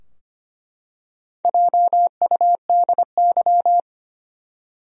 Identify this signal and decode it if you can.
Morse code, heard as a keyed tone, about 25 wpm, 700 Hz. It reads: JUDY
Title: USA Common Names - Max 5 Letters - 25wpm